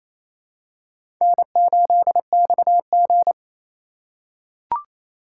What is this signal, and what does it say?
Morse code, heard as a keyed tone, about 28 wpm, 700 Hz. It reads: N8XG
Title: USA Call Signs - 4 Characters - 2x Character Spacing - 28wpm